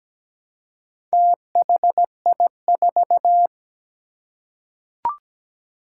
Morse code, heard as a keyed tone, about 17 wpm, 700 Hz. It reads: THI4
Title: Four Challenging Letter-Number Combinations 17wpm